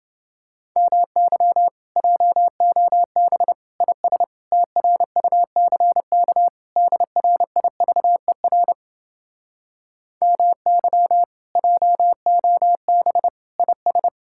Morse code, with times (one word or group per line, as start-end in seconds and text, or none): 0.8-1.7 MY
2.0-3.5 JOB
3.8-4.2 IS
4.5-6.5 TRUCK
6.8-8.7 DRIVER
10.2-11.2 MY
11.6-13.3 JOB
13.6-14.1 IS